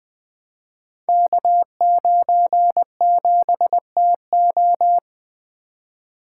Morse code, heard as a keyed tone, about 20 wpm, 700 Hz. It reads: K97TO
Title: Code Groups - 5 Letter:Numbers - 20wpm